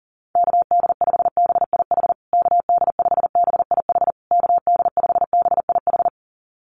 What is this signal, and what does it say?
Morse code, heard as a keyed tone, about 40 wpm, 700 Hz. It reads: KD5BIH KD5BIH KD5BIH
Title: USA Call Signs Repeated 3x 40wpm